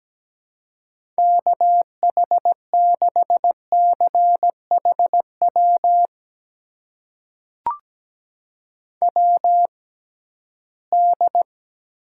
Morse code, heard as a keyed tone, about 17 wpm, spaced as for 13 wpm, 700 Hz. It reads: KH6CHW WD
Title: USA Call Signs - 6 Characters - 6x Character Spacing - 17wpm